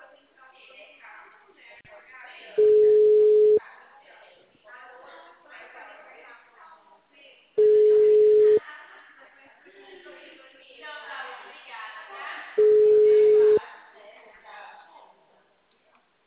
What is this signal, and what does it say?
Ringback tone (Europe, 425 Hz), with speech.
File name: out- -2044- - - .5213